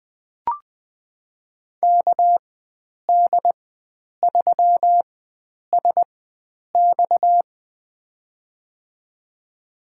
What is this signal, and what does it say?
Morse code, heard as a keyed tone, about 20 wpm, spaced as for 9 wpm, 700 Hz. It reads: KD3SX
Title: USA Call Signs - 5 Characters - 4x Character Spacing - 20wpm